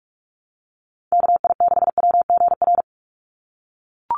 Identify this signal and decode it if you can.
Morse code, heard as a keyed tone, about 44 wpm, 700 Hz. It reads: KI6WGR
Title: USA Call Signs 45wpm